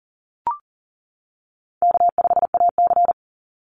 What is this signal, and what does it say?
Morse code, heard as a keyed tone, about 40 wpm, 700 Hz. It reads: K5AC
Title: USA Call Signs 40wpm